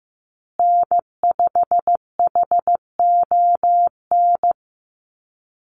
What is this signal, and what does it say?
Morse code, heard as a keyed tone, about 15 wpm, 700 Hz. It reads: N5HON
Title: USA Call Signs Repeated 2x 15wpm